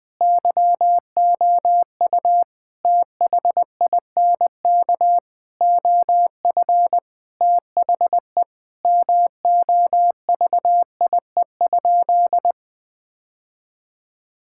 Morse code, heard as a keyed tone, about 21 wpm, 700 Hz. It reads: YOU THINK OF THE MOVIE?
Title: Sentences from Top 1000 Words - Review Part 5 - 20wpm